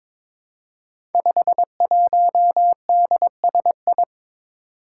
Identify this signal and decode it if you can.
Morse code, heard as a keyed tone, about 23 wpm, 700 Hz. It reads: 51DSI
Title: Five Challenging Letter-Number Combinations 22wpm